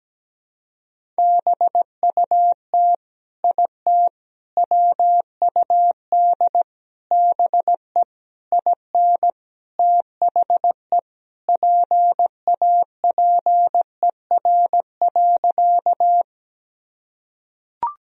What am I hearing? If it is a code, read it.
Morse code, heard as a keyed tone, about 17 wpm, 700 Hz. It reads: BUT IT WUD BE IN THE PAPER.
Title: Sentences from Top 300 Words and Abbreviations 17wpm